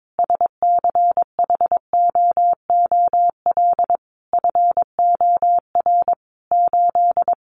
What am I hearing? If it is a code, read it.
Morse code, heard as a keyed tone, about 22 wpm, 700 Hz. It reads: SCHOOL FOR 8
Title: Additional Easy Phrases 22wpm